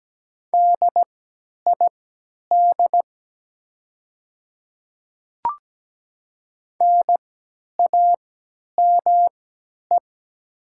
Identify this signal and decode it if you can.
Morse code, heard as a keyed tone, about 17 wpm, spaced as for 10 wpm, 690 Hz. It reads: DID NAME
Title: 3x Character Spacing - Top 300 Words - 17wpm